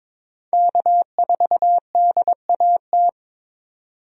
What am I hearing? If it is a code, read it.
Morse code, heard as a keyed tone, about 21 wpm, 700 Hz. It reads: K4DAT
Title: USA Call Signs - 5 Characters - 22wpm